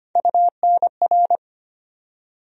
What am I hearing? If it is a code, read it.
Morse code, heard as a keyed tone, about 25 wpm, 700 Hz. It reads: UNR
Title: USA Call Sign Postfixes - 3 Letters 25wpm